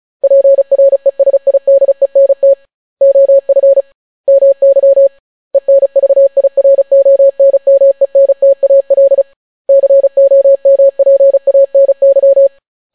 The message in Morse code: PRESIDENT OF MY ENVIRONMENTAL COMPANY

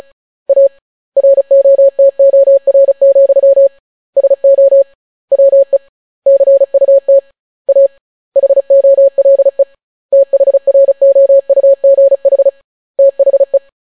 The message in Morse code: A ROTOR, SO WE CUT A HOLE THROUGH THE

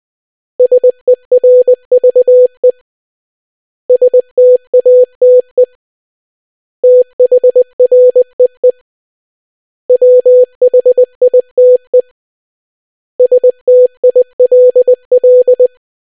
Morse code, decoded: SERVE STATE THREE WHITE STILL

